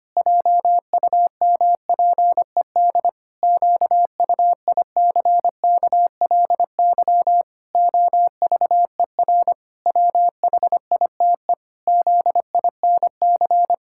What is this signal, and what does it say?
Morse code, heard as a keyed tone, about 25 wpm, 700 Hz. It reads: JUMPED QUICKLY OVER WHITE ZINC
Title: Pangram Challenge v2 - 25wpm